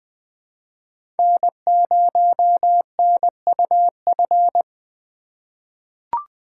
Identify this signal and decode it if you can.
Morse code, heard as a keyed tone, about 21 wpm, 700 Hz. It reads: N0NUF
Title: USA Call Signs - 5 Characters - 4x Character Spacing - 20wpm